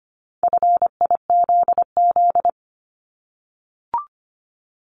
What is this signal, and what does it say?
Morse code, heard as a keyed tone, about 25 wpm, 700 Hz. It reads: FIZZ
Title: Multi-letter Phonemes Repeated 3x 25wpm